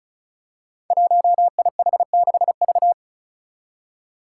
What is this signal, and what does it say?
Morse code, heard as a keyed tone, about 36 wpm, 700 Hz. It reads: 1IH6V
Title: Five Challenging Letter-Number Combinations 35wpm